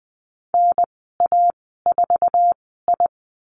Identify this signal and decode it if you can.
Morse code, heard as a keyed tone, about 20 wpm, spaced as for 15 wpm, 700 Hz. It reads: NA4I
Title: 088 Lesson 15g - Number 5 - Call Signs 20wpm